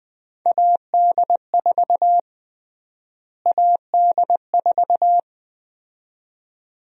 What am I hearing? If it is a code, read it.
Morse code, heard as a keyed tone, about 21 wpm, 700 Hz. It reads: AD4 AD4
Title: USA Call Sign Prefixes - Repeated 2x - 20wpm